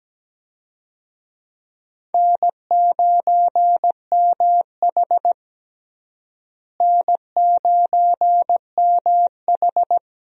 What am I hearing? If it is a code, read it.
Morse code, heard as a keyed tone, about 17 wpm, 700 Hz. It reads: N9MH N9MH
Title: USA Call Signs - 4 Characters - Repeated 3x - 17wpm